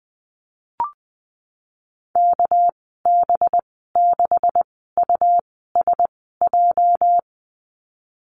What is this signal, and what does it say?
Morse code, heard as a keyed tone, about 20 wpm, spaced as for 15 wpm, 700 Hz. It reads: KB6USJ
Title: 219 Lesson 35g - Number 0 - Call Signs 20wpm